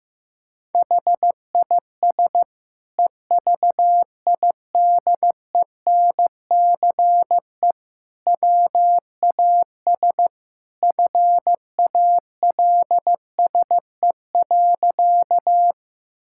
Morse code, heard as a keyed tone, about 16 wpm, 700 Hz. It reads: HIS EVIDENCE WAS FALSE.
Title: Sentences from Top 1600 Words 15wpm